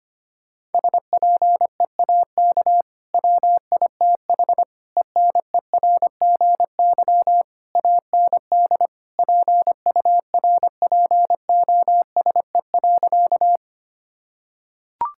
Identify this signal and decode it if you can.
Morse code, heard as a keyed tone, about 25 wpm, 700 Hz. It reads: SPEAK WITH ENERGY AND PURPOSE.
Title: Sentences from Top 1000 Words - Review Part 6 - 25wpm